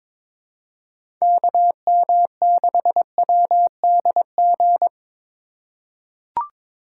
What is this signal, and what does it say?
Morse code, heard as a keyed tone, about 22 wpm, 700 Hz. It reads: KM6WDG
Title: USA Call Signs - 6 Characters - 4x Character Spacing - 22wpm